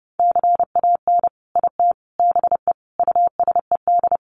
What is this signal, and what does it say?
Morse code, heard as a keyed tone, about 30 wpm, 700 Hz. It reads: CAN IT BE USED
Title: Sentences from Top 600 Words 30wpm